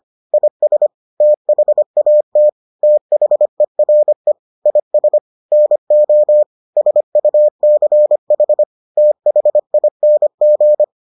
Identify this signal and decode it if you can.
Morse code, heard as a keyed tone, about 25 wpm, 600 Hz. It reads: IS THAT THERE IS NO SUCH THING